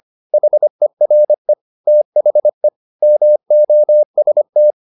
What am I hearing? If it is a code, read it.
Morse code, heard as a keyed tone, about 25 wpm, 600 Hz. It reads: HERE THE MOST
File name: the old man and the sea part 1 25wpm